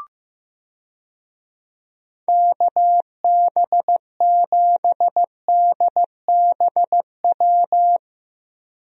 Morse code, heard as a keyed tone, about 15 wpm, 700 Hz. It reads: KB7DBW
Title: USA Call Signs - 6 Characters - 15wpm